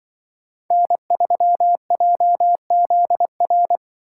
Morse code, heard as a keyed tone, about 24 wpm, 700 Hz. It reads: N3JZR
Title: Speed Racing - USA 1x3 Call Signs 20wpm